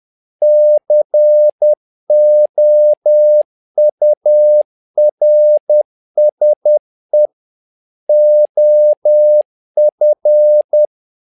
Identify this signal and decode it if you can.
Morse code, heard as a keyed tone, about 10 wpm, 600 Hz. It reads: COURSE OF
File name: learning center / tower trouble / tower trouble 10wpm